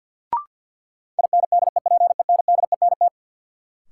Morse code, heard as a keyed tone, about 53 wpm, 700 Hz. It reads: INDEPENDENT
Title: Sentences from Top 900 Words - New Word Drill - 50wpm